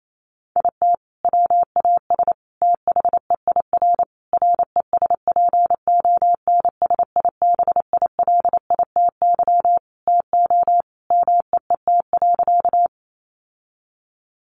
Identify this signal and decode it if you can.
Morse code, heard as a keyed tone, about 28 wpm, 700 Hz. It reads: IT WAS THEIR RESPONSIBILITY TO MEET.
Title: Sentences from Top 1200 Words 28wpm